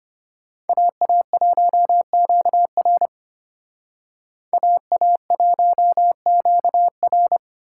Morse code, heard as a keyed tone, about 29 wpm, 700 Hz. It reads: AA1QR AA1QR
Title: Speed Racing - USA 2x2 Call Signs 25wpm